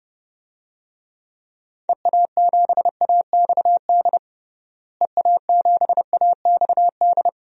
Morse code, heard as a keyed tone, about 30 wpm, 700 Hz. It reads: EA7AXD EA7AXD
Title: DXCC-100 Call Signs - 6 Characters - Repeated 3x - 30wpm